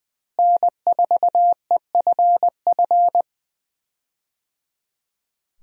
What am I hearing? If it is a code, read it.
Morse code, heard as a keyed tone, about 20 wpm, 700 Hz. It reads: N4EFF
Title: USA Call Signs - 5 Characters - Repeated 3x - 20wpm